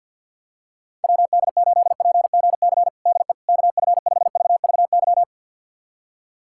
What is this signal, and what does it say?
Morse code, heard as a keyed tone, about 53 wpm, 700 Hz. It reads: WD8PGC DE KF5VVY
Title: USA Callsign DE Callsign - 6 Characters - 50wpm